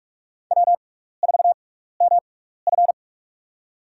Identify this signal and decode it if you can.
Morse code, heard as a keyed tone, about 46 wpm, spaced as for 16 wpm, 700 Hz. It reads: W4MF
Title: USA Call Signs - 6x Character Spacing - 45wpm